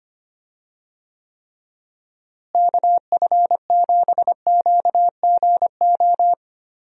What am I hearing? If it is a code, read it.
Morse code, heard as a keyed tone, about 25 wpm, 700 Hz. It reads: KF7QGO
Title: USA Call Signs - 6 Characters - Repeated 2x - 25wpm